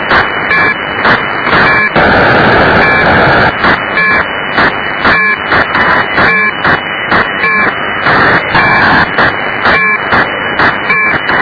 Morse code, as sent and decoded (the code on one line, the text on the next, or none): none